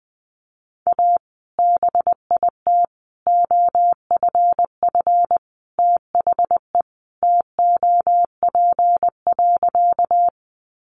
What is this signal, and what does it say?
Morse code, heard as a keyed tone, about 20 wpm, 700 Hz. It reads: A BIT OFF THE TOP.
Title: Sentences from Top 900 Words 20wpm